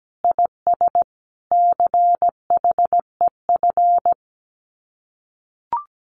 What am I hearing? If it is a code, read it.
Morse code, heard as a keyed tone, about 17 wpm, 700 Hz. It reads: IS CHEF
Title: Common QSO Phrases Encore 17wpm